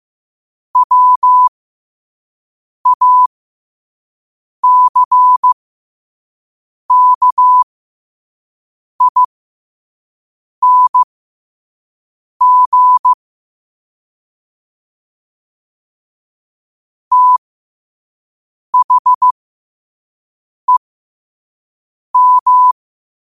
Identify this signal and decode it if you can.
Morse code, heard as a keyed tone, about 15 wpm, spaced as for 5 wpm, 1000 Hz. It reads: WACKING THEM
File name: qotdfiles / 5wpm